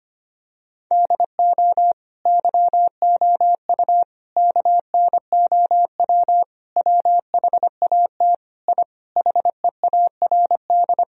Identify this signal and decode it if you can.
Morse code, heard as a keyed tone, about 25 wpm, 700 Hz. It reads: DO YOU KNOW WHAT I HEARD
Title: Sentences from Top 1000 Words - Review Part 4 - 25wpm